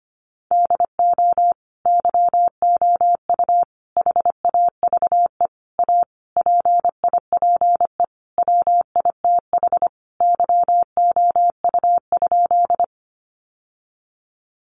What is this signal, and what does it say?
Morse code, heard as a keyed tone, about 25 wpm, 700 Hz. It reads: DO YOU HAVE A PIPE WITH YOU?
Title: Sentences from Top 1500 Words 25wpm